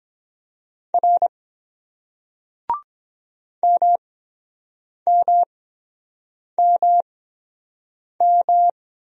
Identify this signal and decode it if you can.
Morse code, heard as a keyed tone, about 22 wpm, spaced as for 7 wpm, 700 Hz. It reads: R MMMM